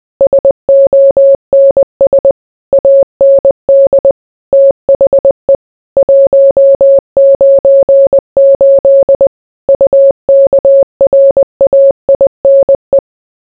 Morse code, heard as a keyed tone, about 20 wpm, 550 Hz. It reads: SONS AND THE 198 UKRAINE